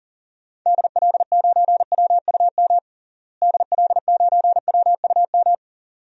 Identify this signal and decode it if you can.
Morse code, heard as a keyed tone, about 40 wpm, 700 Hz. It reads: DL9WUM DL9WUM
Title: DXCC-100 Call Signs - 6 Characters - Repeated 2x - 40wpm